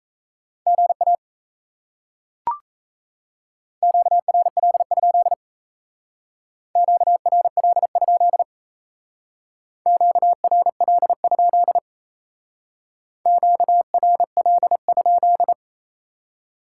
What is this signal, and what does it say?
Morse code, heard as a keyed tone, about 35 wpm, 700 Hz. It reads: GA QRL? QRL? QRL? QRL?